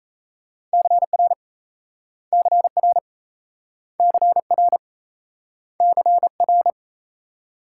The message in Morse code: CR CR CR CR

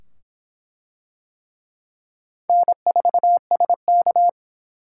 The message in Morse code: N4SK

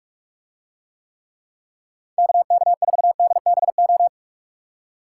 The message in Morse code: KK4DBO

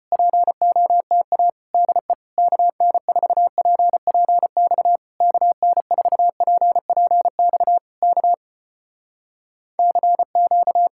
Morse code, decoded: POTA DE KN4PPX KN4PPX K CQ

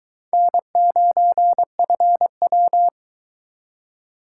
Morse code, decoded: N9FW

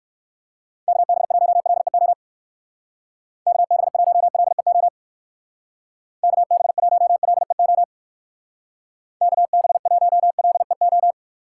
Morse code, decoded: KB1LEO KB1LEO KB1LEO KB1LEO

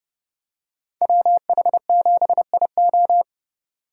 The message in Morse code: WH7IO

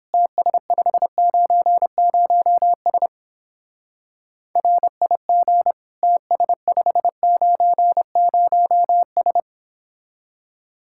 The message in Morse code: TS590S RIG TS590S

